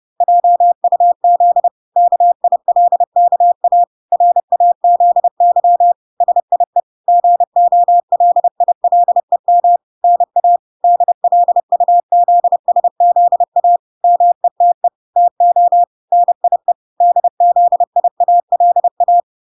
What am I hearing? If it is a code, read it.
Morse code, heard as a keyed tone, about 30 wpm, 700 Hz. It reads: JUZ KILKA RAZY SIE GOLILEM NA DLUZSZA METE TO NIE DZIALA